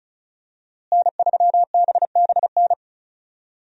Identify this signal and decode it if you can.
Morse code, heard as a keyed tone, about 34 wpm, 700 Hz. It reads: N3BBN
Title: USA Call Signs - 5 Characters - Repeated 2x - 35wpm